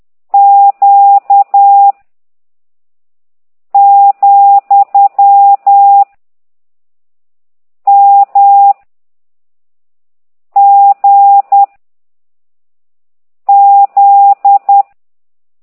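Morse code, heard as a keyed tone, about 10 wpm, 800 Hz. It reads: Q , M G Z
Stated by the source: Morse code lesson 5 exercise 10 by CT2IDW, containing only characters M, O, G, Z, Q and comma.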